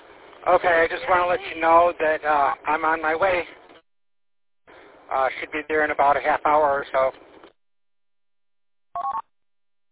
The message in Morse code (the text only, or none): none